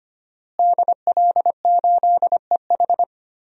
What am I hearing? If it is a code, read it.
Morse code, heard as a keyed tone, about 25 wpm, 700 Hz. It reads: DL8EH
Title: DXCC-100 Call Signs - 5 Characters - Repeated 2x - 25wpm